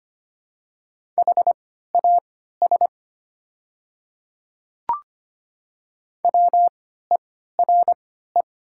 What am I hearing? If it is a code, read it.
Morse code, heard as a keyed tone, about 25 wpm, spaced as for 14 wpm, 700 Hz. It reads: HAS WERE
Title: 3x Character Spacing - Top 100 Words - 25wpm